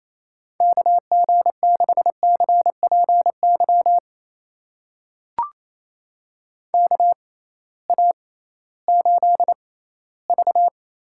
Morse code, heard as a keyed tone, about 28 wpm, spaced as for 16 wpm, 700 Hz. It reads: KG6CPY KA8V
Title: USA Call Signs - 6 Characters - 6x Character Spacing - 28wpm